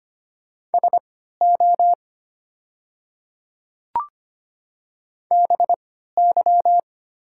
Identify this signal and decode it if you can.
Morse code, heard as a keyed tone, about 25 wpm, spaced as for 14 wpm, 700 Hz. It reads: SO BY
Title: 3x Character Spacing - Top 200 Words - 25wpm